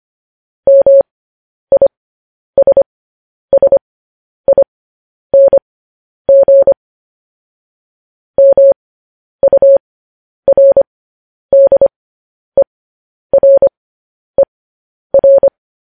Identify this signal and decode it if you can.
Morse code, heard as a keyed tone, about 25 wpm, spaced as for 10 wpm, 550 Hz. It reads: MISSING MURDERER
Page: news headlines in morse code at 10 wpm